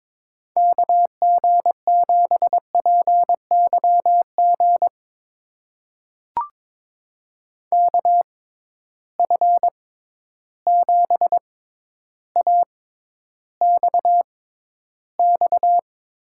Morse code, heard as a keyed tone, about 22 wpm, spaced as for 11 wpm, 700 Hz. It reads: KG7PYG KF7AXX
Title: USA Call Signs - 6 Characters - 6x Character Spacing - 22wpm